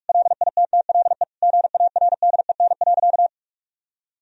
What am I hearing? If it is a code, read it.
Morse code, heard as a keyed tone, about 46 wpm, 700 Hz. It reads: LITTLE GARDEN.
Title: Sentences from Top 1000 Words - Review Part 4 - 45wpm